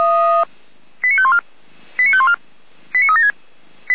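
Five-tone selective call (ZVEI), heard as quick runs of five tones, tones of about 70 ms. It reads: 89514 89514 89376
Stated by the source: ZVEI five-tone call received with an SDR.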